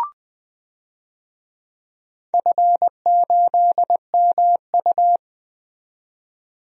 Morse code, heard as a keyed tone, about 20 wpm, 700 Hz. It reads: F8MU
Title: DXCC-100 Call Signs - 4 Characters - 20wpm